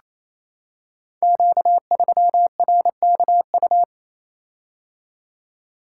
Morse code, heard as a keyed tone, about 27 wpm, 700 Hz. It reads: Q3RKU
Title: Code Groups - 5 Letter:Numbers - 28wpm